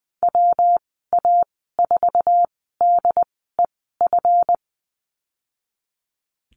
Morse code, heard as a keyed tone, about 20 wpm, spaced as for 15 wpm, 700 Hz. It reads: WA4DEF
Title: 186 Lesson 30g - Letter b - Call Signs 20wpm